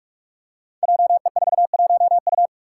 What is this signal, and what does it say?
Morse code, heard as a keyed tone, about 42 wpm, 700 Hz. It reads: JE41U
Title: Five Challenging Letter-Number Combinations 45wpm